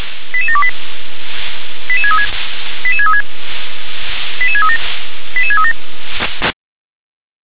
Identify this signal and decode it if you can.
Five-tone selective call (ZVEI), heard as quick runs of five tones, tones of about 70 ms. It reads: 88728 88636 88636 88637 88637
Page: /357/SDR/ZVEI/88730/2023.06.03/